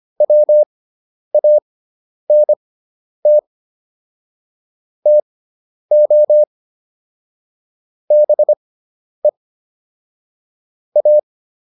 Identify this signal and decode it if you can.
Morse code, heard as a keyed tone, about 25 wpm, spaced as for 10 wpm, 600 Hz. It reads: WANT TO BE A
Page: intermediate-practice-files